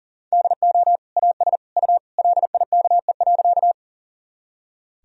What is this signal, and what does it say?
Morse code, heard as a keyed tone, about 40 wpm, 700 Hz. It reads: DO AS U LIKE.
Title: Sentences from Top 100 Words and Abbreviations 40wpm